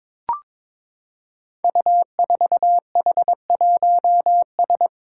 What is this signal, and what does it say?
Morse code, heard as a keyed tone, about 22 wpm, 700 Hz. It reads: U4H1S
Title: Five Challenging Letter-Number Combinations 22wpm